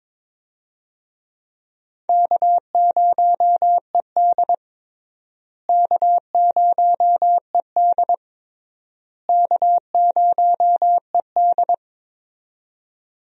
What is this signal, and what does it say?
Morse code, heard as a keyed tone, about 21 wpm, 700 Hz. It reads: K0ED K0ED K0ED
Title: USA Call Signs - 4 Characters - Repeated 3x - 22wpm